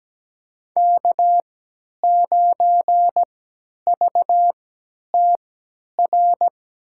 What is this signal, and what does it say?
Morse code, heard as a keyed tone, about 17 wpm, spaced as for 10 wpm, 700 Hz. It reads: K9VTR
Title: USA Call Signs - 5 Characters - 3x Character Spacing - 17wpm